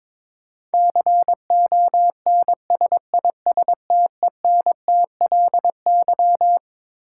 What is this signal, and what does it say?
Morse code, heard as a keyed tone, about 22 wpm, 700 Hz. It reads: CONSISTENTLY
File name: Twelve Letter Words - 22wpm